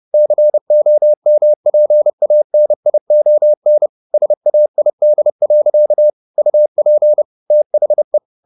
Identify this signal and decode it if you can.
Morse code, heard as a keyed tone, about 30 wpm, 600 Hz. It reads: COMPANION SAID. UP THE